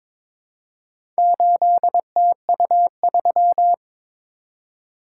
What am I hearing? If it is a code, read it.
Morse code, heard as a keyed tone, about 22 wpm, 690 Hz. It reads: 8TU3
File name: practice / 22wpm / Four Letter-Number Combinations - 22wpm